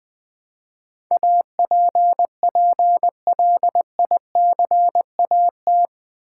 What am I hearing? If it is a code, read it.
Morse code, heard as a keyed tone, about 20 wpm, 700 Hz. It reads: APPLICAT
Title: Build A Word - 4 Letters At A Time - 11-Letter Words - 20wpm